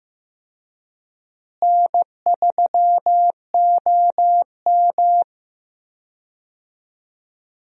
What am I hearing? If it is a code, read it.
Morse code, heard as a keyed tone, about 16 wpm, 700 Hz. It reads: N3OM
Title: USA Call Signs - 4 Characters - Repeated 3x - 15wpm